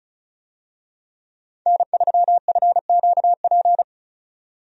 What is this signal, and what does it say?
Morse code, heard as a keyed tone, about 36 wpm, 700 Hz. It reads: N3FQP
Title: USA Call Signs - 5 Characters - 35wpm